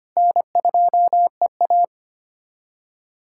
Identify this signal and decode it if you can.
Morse code, heard as a keyed tone, about 25 wpm, 700 Hz. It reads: N2EA
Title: USA Callsign DE Callsign - 4 Characters - 25wpm